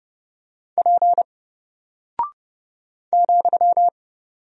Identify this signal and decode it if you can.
Morse code, heard as a keyed tone, about 30 wpm, 700 Hz. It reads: P ,
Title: Single Character (Letter:Number:Punctuation:Prosign) Alt 30wpm